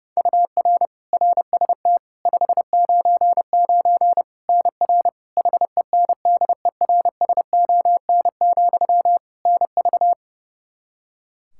Morse code, heard as a keyed tone, about 31 wpm, 700 Hz. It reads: UR RST 599 NR HENDERSON, NV
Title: Practice QSO - Exchange 1 - Beginning - 30wpm